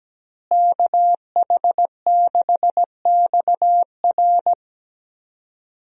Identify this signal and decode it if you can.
Morse code, heard as a keyed tone, about 17 wpm, 700 Hz. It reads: KH6XR
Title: DXCC-100 Call Signs - 5 Characters - Repeated 3x - 17wpm